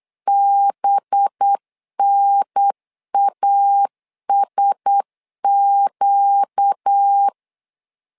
Morse code, heard as a keyed tone, about 8 wpm, 800 Hz. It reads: BNASQ
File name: LEZIONI / 5 LBVC / 5 LB 8wpm 2816